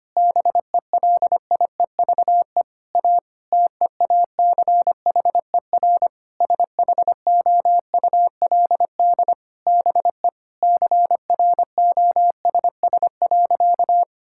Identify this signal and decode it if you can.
Morse code, heard as a keyed tone, about 25 wpm, 700 Hz. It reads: BELIEVE A TEACHER SHOULD BE CROSS.
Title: Sentences from Top 1000 Words - Review Part 6 - 25wpm